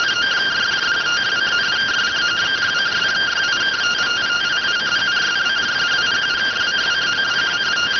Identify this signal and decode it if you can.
Two-tone teleprinter signal, about 170 Hz shift.